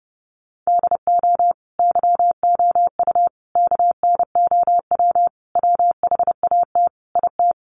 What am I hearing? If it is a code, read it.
Morse code, heard as a keyed tone, about 30 wpm, 700 Hz. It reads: DO YOU KNOW WHAT IT
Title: Sentences from Top 100 Words 30wpm